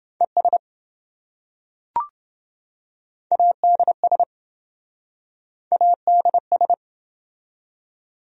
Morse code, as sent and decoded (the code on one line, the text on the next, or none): . ... / .- -.. ... / .- -.. ...
ES ADS ADS